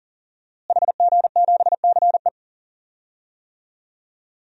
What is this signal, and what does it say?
Morse code, heard as a keyed tone, about 37 wpm, 700 Hz. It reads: HG7CE
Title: DXCC-100 Call Signs - 5 Characters - Repeated 3x - 40wpm